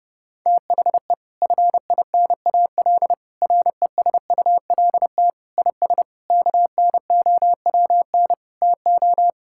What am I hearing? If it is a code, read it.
Morse code, heard as a keyed tone, about 30 wpm, 700 Hz. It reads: THE FINAL RESULT IS KNOWN TO
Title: Sentences from Top 1000 Words - Review Part 5 - 30wpm